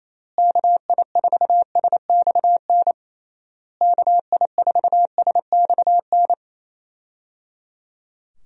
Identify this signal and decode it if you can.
Morse code, heard as a keyed tone, about 27 wpm, 700 Hz. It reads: KI4SXN KI4SXN
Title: USA Call Signs - 6 Characters - Repeated 3x - 28wpm